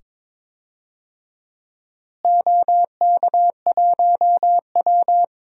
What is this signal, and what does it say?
Morse code, heard as a keyed tone, about 22 wpm, 700 Hz. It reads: OK1W
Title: DXCC-100 Call Signs - 4 Characters - Repeated 2x - 22wpm